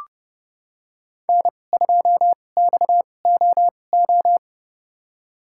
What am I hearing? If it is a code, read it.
Morse code, heard as a keyed tone, about 30 wpm, spaced as for 22 wpm, 700 Hz. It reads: N2XOO
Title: USA Call Signs - 5 Characters - 2x Character Spacing - 30wpm